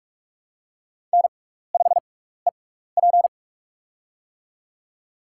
Morse code, heard as a keyed tone, about 44 wpm, spaced as for 15 wpm, 700 Hz. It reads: N5EP
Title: USA Call Signs - 4 Characters - 6x Character Spacing - 45wpm